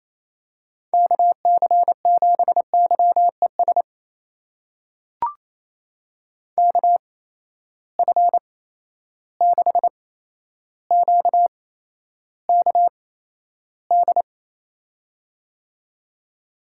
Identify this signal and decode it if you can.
Morse code, heard as a keyed tone, about 28 wpm, spaced as for 12 wpm, 700 Hz. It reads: KC7YES KF6QKD
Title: USA Call Signs - 6 Characters - 8x Character Spacing - 28wpm